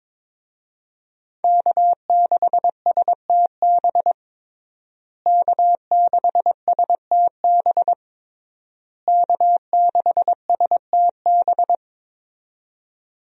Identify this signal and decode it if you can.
Morse code, heard as a keyed tone, about 22 wpm, 700 Hz. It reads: K6STB K6STB K6STB
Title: USA Call Signs - 5 Characters - Repeated 3x - 22wpm